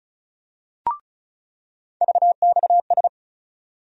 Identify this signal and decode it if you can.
Morse code, heard as a keyed tone, about 36 wpm, 700 Hz. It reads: VXS
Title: USA Call Sign Postfixes - 3 Letters 35wpm